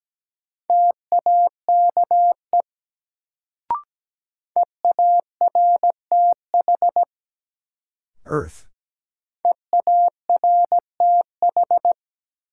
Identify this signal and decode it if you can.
Morse code, heard as a keyed tone, about 17 wpm, 700 Hz. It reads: TAKE EARTH EARTH
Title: Sentences from Top 200 Words - New Word Drill - 17wpm